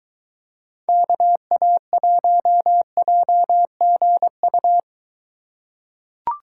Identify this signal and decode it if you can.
Morse code, heard as a keyed tone, about 23 wpm, 700 Hz. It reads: KA1JGU